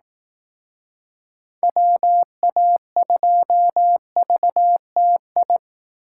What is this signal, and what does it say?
Morse code, heard as a keyed tone, about 18 wpm, 700 Hz. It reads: WA2VTI
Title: Speed Racing - USA 2x3 Call Signs 15wpm